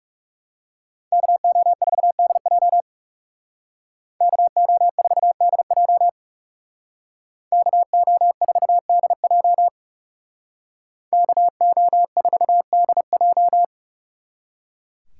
Morse code, heard as a keyed tone, about 36 wpm, 700 Hz. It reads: KO4DJ KO4DJ KO4DJ KO4DJ
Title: Speed Racing - USA 2x2 Call Signs 30wpm